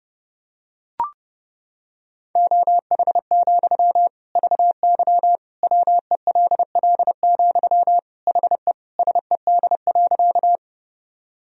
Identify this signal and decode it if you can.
Morse code, heard as a keyed tone, about 30 wpm, 700 Hz. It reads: OH, VY WELL, HE SED.
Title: Sentences from Top 500 Words and Abbreviations 30wpm